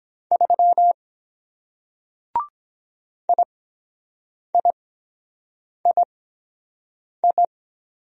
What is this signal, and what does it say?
Morse code, heard as a keyed tone, about 23 wpm, spaced as for 7 wpm, 700 Hz. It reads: 3 IIII